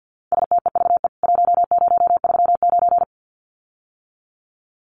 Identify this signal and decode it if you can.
Morse code, heard as a keyed tone, about 50 wpm, 700 Hz. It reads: STEVE 1039
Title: CWT Exchange 50wpm